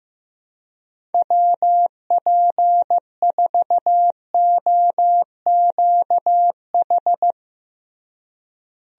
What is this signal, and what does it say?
Morse code, heard as a keyed tone, about 15 wpm, 700 Hz. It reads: WP4OQH
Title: USA Call Signs - 6 Characters - Repeated 2x - 15wpm